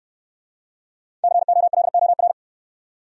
Morse code, heard as a keyed tone, about 66 wpm, 700 Hz. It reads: COLOR